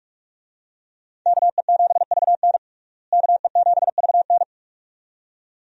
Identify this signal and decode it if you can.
Morse code, heard as a keyed tone, about 43 wpm, 700 Hz. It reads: KE7VN KE7VN
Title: USA Call Signs - 5 Characters - Repeated 2x - 45wpm